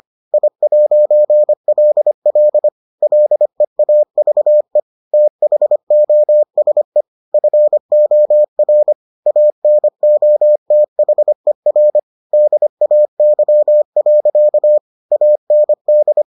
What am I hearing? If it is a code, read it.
Morse code, heard as a keyed tone, about 25 wpm, 600 Hz. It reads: I'LL LEAVE THOSE FOR ANOTHER DAY. AND